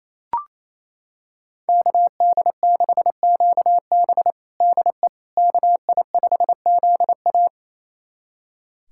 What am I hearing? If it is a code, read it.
Morse code, heard as a keyed tone, about 28 wpm, 700 Hz. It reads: KD6QB DE KI5ZA
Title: USA Callsign DE Callsign - 5 Characters - 28wpm